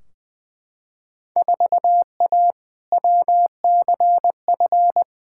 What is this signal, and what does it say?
Morse code, heard as a keyed tone, about 21 wpm, 700 Hz. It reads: 4A WCF
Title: Field Day Exchanges - 20wpm